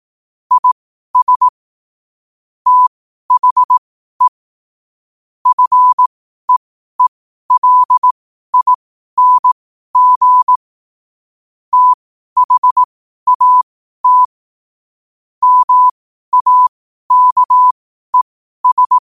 Morse code, heard as a keyed tone, about 18 wpm, spaced as for 12 wpm, 1000 Hz. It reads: IS THE FEELING THAT MAKES
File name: qotdfiles / 12wpm